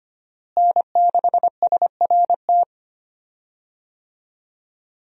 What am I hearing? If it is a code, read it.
Morse code, heard as a keyed tone, about 25 wpm, 700 Hz. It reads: N6SRT
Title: USA Call Signs - 5 Characters - Repeated 2x - 25wpm